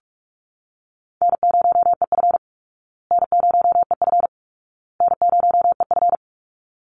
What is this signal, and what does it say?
Morse code, heard as a keyed tone, about 44 wpm, 700 Hz. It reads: N0EF N0EF N0EF
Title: Call Signs Repeated 2x - 45wpm